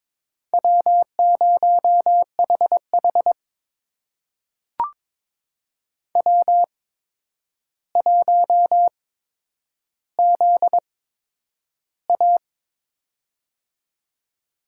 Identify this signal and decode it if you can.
Morse code, heard as a keyed tone, about 22 wpm, spaced as for 9 wpm, 700 Hz. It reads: W0HH W1ZA
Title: USA Call Signs - 4 Characters - 8x Character Spacing - 22wpm